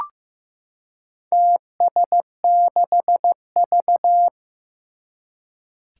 Morse code, heard as a keyed tone, about 15 wpm, 700 Hz. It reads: TS6V
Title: Four Challenging Letter-Number Combinations 15wpm